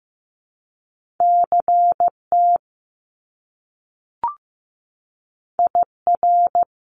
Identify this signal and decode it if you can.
Morse code, heard as a keyed tone, about 15 wpm, 700 Hz. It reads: CT IR
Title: Top Two Letter Combinations in Words 15wpm